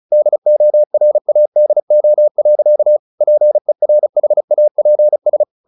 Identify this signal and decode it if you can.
Morse code, heard as a keyed tone, about 35 wpm, 600 Hz. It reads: DORADO. PERHAPS
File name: the old man and the sea 2 35wpm